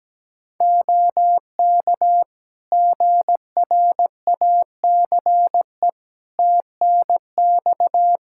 Morse code, heard as a keyed tone, about 17 wpm, 700 Hz. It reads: OK GRACE TNX
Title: Practice QSO - Exchange 4 - Ending - 17wpm